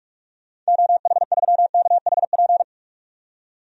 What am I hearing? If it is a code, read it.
Morse code, heard as a keyed tone, about 42 wpm, 700 Hz. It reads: OH3KHP
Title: DXCC-100 Call Signs - 6 Characters - 45wpm